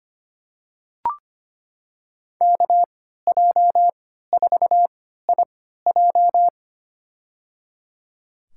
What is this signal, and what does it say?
Morse code, heard as a keyed tone, about 25 wpm, spaced as for 14 wpm, 700 Hz. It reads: KJ4IJ
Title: USA Call Signs - 5 Characters - 3x Character Spacing - 25wpm